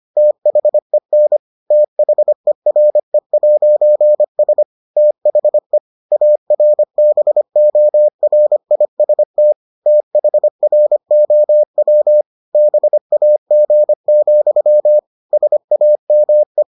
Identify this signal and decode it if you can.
Morse code, heard as a keyed tone, about 25 wpm, 600 Hz. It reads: THEN THERE'S THE ARBORIST THROW BAG, SAME